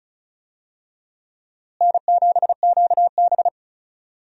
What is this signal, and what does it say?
Morse code, heard as a keyed tone, about 36 wpm, 700 Hz. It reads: N7QB
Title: USA Call Signs - 4 Characters - Repeated 3x - 35wpm